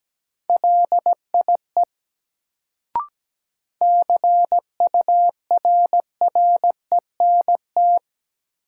Morse code, heard as a keyed tone, about 17 wpm, 700 Hz. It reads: LIE CURRENT
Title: Sentences from Top 1600 Words - New Word Drill Alt - 17wpm